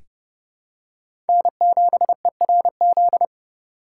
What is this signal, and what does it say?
Morse code, heard as a keyed tone, about 32 wpm, 700 Hz. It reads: N7ERZ
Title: USA Call Signs - 5 Characters - 4x Character Spacing - 30wpm